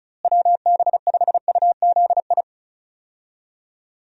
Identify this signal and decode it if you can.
Morse code, heard as a keyed tone, about 34 wpm, 700 Hz. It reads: WB5UZI
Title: USA Call Signs - 6 Characters - Repeated 3x - 35wpm